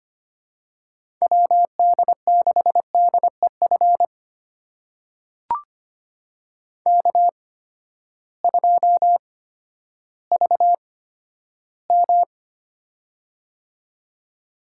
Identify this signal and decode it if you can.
Morse code, heard as a keyed tone, about 25 wpm, spaced as for 13 wpm, 700 Hz. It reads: WD6DEF K2VM
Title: USA Call Signs - 8x Character Spacing - 25wpm